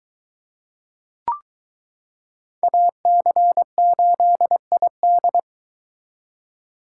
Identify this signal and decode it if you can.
Morse code, heard as a keyed tone, about 22 wpm, 700 Hz. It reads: AC8ID